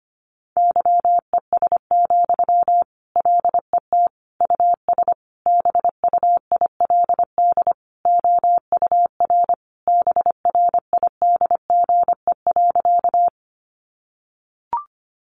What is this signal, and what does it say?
Morse code, heard as a keyed tone, about 25 wpm, 700 Hz. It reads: YES, LET US BUILD OUR BRIDGE.
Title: Sentences from Top 1200 Words 25wpm